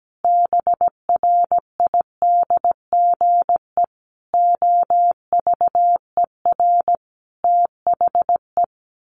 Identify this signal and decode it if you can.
Morse code, heard as a keyed tone, about 17 wpm, 700 Hz. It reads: BRIDGE OVER THE
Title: Sentences from Top 1500 Words 17wpm